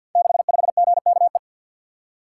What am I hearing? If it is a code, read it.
Morse code, heard as a keyed tone, about 50 wpm, 700 Hz. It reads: 65GKE
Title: Code Groups - 5 Letter:Numbers - 50wpm